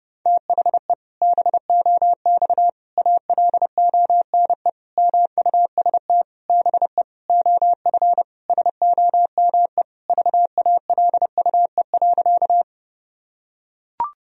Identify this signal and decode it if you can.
Morse code, heard as a keyed tone, about 30 wpm, 700 Hz. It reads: THE BOX ALONE MUST BE OF SOME VALUE.
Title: Sentences from Top 1000 Words - Review Part 3 - 30wpm